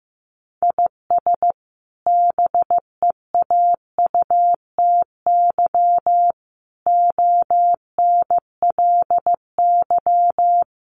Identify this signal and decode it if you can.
Morse code, heard as a keyed tone, about 15 wpm, 700 Hz. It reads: IS BEAUTY ONLY
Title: Sentences from Top 1300 Words 15wpm